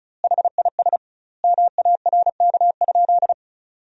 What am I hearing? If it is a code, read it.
Morse code, heard as a keyed tone, about 34 wpm, 700 Hz. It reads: HIS MARK?
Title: 2x Word-Spacing - Sentences from Top 300 Words - 35wpm